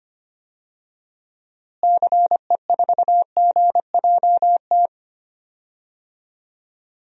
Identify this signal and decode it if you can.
Morse code, heard as a keyed tone, about 25 wpm, 700 Hz. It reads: CE4GJT
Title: DXCC-100 Call Signs - 6 Characters - Repeated 3x - 25wpm